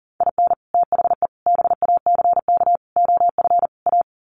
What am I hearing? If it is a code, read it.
Morse code, heard as a keyed tone, about 37 wpm, 700 Hz. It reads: IN THE BACK OF A
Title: Sets of 5 Words Encore2 40wpm